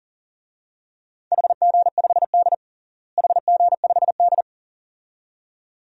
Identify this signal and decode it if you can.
Morse code, heard as a keyed tone, about 37 wpm, 700 Hz. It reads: HG5D HG5D
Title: DXCC-100 Call Signs - 4 Characters - Repeated 2x - 40wpm